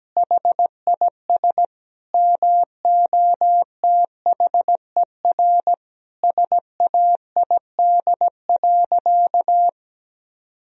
Morse code, heard as a keyed tone, about 17 wpm, 700 Hz. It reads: HIS MOTHER SAID.
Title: Sentences from Top 1000 Words - Review Part 3 - 17wpm